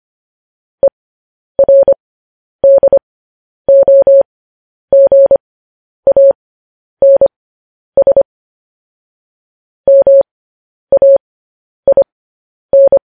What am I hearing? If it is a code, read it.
Morse code, heard as a keyed tone, about 25 wpm, spaced as for 10 wpm, 550 Hz. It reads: ERDOGANS MAIN
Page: news-headlines-in-morse-code-at-10-wpm